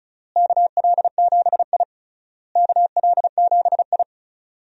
Morse code, heard as a keyed tone, about 34 wpm, 700 Hz. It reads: KL7I KL7I
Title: DXCC-100 Call Signs - 4 Characters - Repeated 3x - 35wpm